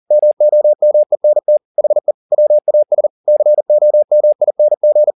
Morse code, heard as a keyed tone, about 40 wpm, 600 Hz. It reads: MOMENT HE WAS COMING